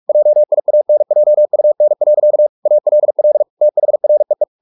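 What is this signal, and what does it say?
Morse code, heard as a keyed tone, about 44 wpm, 600 Hz. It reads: JIANJUN. ALL THREE
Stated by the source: Morse practice at 45 wpm, the novel Never by Ken Follett in English, from CW med Gnister.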